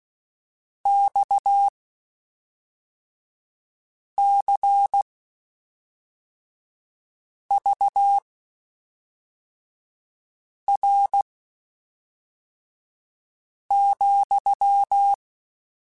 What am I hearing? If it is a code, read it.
Morse code, heard as a keyed tone, about 16 wpm, spaced as for 3 wpm, 800 Hz. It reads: XCVR,
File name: QSOs / QSOs-mp3s / WD0FBT 1@5